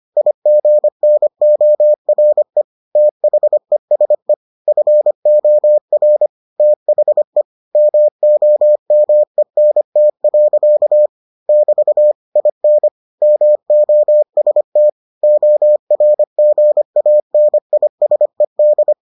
Morse code, heard as a keyed tone, about 25 wpm, 600 Hz. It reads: IGNORE THESE FOR THE MOMENT. <BT> IN MOST ORGANISED